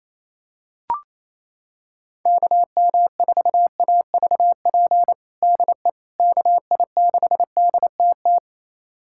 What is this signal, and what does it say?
Morse code, heard as a keyed tone, about 28 wpm, 700 Hz. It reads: KM4AVP DE KI6DTT
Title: USA Callsign DE Callsign - 6 Characters - 28wpm